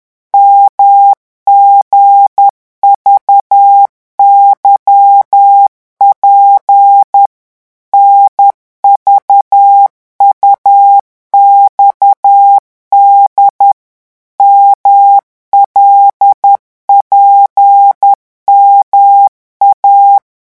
Morse code, heard as a keyed tone, about 10 wpm, 800 Hz. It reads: MGVYP NVUXD MLPMA
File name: morse mp3 / revision5-vitesse 10 mots